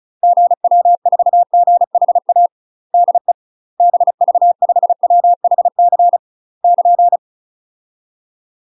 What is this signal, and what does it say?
Morse code, heard as a keyed tone, about 36 wpm, 700 Hz. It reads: GW4GHA DE BV5WHC <KN>
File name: UK MASTER QSOs TRAINING / 35WPM / UK MASTER QSOs 0038